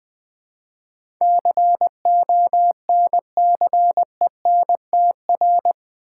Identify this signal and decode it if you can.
Morse code, heard as a keyed tone, about 20 wpm, 700 Hz. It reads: CONCENTR
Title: Build A Word - 4 Letters At A Time - 11-Letter Words - 20wpm